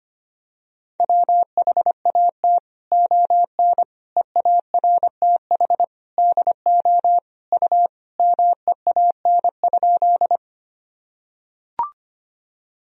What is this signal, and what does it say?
Morse code, heard as a keyed tone, about 25 wpm, 700 Hz. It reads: WHAT ON EARTH DO U MEAN?
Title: Sentences from Top 200 Words and Abbreviations 25wpm